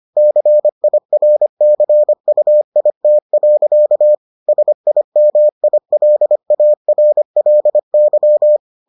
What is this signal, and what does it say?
Morse code, heard as a keyed tone, about 25 wpm, 600 Hz. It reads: CIRCUIT. SIMILARLY